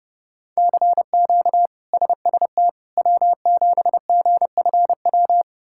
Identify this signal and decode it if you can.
Morse code, heard as a keyed tone, about 30 wpm, 700 Hz. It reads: CQ SST W7GFW
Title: K1USN SST QSO Practice - 30wpm